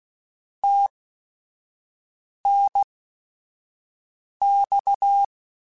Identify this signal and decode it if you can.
Morse code, heard as a keyed tone, about 16 wpm, spaced as for 5 wpm, 800 Hz. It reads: TNX